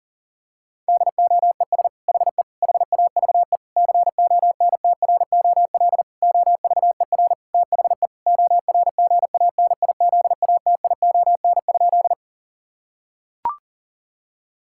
Morse code, heard as a keyed tone, about 40 wpm, 700 Hz. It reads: DOES HE HAVE CONTROL OVER THE ORGANIZATION?
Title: Sentences from Top 1000 Words - Review Part 5 - 40wpm